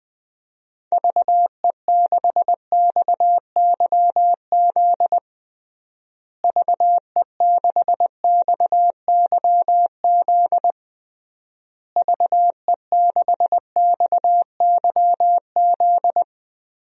Morse code, heard as a keyed tone, about 19 wpm, 700 Hz. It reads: VE6XYZ VE6XYZ VE6XYZ
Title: DXCC-100 Call Signs - 6 Characters - Repeated 3x - 20wpm